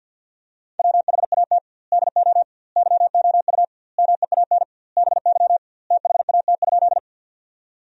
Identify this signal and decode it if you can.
Morse code, heard as a keyed tone, about 50 wpm, 700 Hz. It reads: WHAT DO YOU MEAN BY THAT?
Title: 2x Word-Spacing - Sentences from Top 200 Words - 50wpm